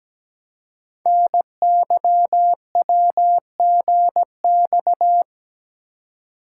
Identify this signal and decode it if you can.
Morse code, heard as a keyed tone, about 17 wpm, 700 Hz. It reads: NYWGX
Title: Code Groups - 5 Letter:Numbers - 17wpm